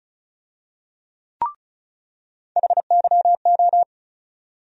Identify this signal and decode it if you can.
Morse code, heard as a keyed tone, about 36 wpm, 700 Hz. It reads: HYO
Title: USA Call Sign Postfixes - 3 Letters 35wpm